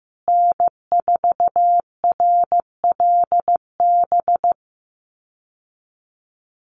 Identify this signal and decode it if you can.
Morse code, heard as a keyed tone, about 15 wpm, 700 Hz. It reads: N4RLB
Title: USA Call Signs Repeated 2x 15wpm